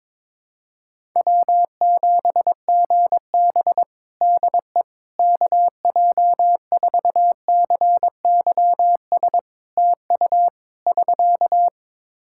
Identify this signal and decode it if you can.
Morse code, heard as a keyed tone, about 22 wpm, 700 Hz. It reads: W7GB DE KJ4CYS TU <SK>
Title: Practice QSO - Exchange 4 - Ending - 22wpm